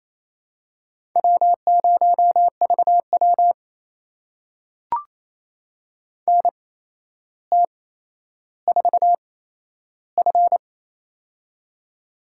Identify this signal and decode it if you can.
Morse code, heard as a keyed tone, about 27 wpm, spaced as for 12 wpm, 700 Hz. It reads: W0VW NT4F
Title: USA Call Signs - 4 Characters - 8x Character Spacing - 28wpm